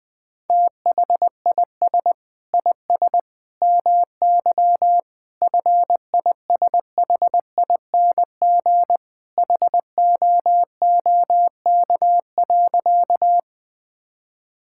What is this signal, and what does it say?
Morse code, heard as a keyed tone, about 21 wpm, 700 Hz. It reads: THIS IS MY FISHING HOOK.
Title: Sentences from Top 1600 Words 20wpm